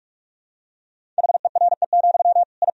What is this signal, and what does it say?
Morse code, heard as a keyed tone, about 41 wpm, 700 Hz. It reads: HERE, I
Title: Sentences from Top 1000 Words - Review Part 1 - 45wpm